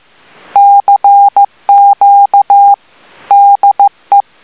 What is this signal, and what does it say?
Morse code, heard as a keyed tone, about 15 wpm, 800 Hz. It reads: CQ DE